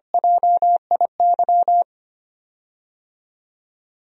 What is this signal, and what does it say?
Morse code, heard as a keyed tone, about 25 wpm, 700 Hz. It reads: JIY
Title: USA Call Signs - 6 Characters - 25wpm